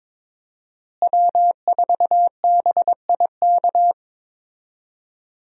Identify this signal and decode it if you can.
Morse code, heard as a keyed tone, about 22 wpm, 700 Hz. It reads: W4BIK
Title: USA Call Signs - 5 Characters - Repeated 2x - 22wpm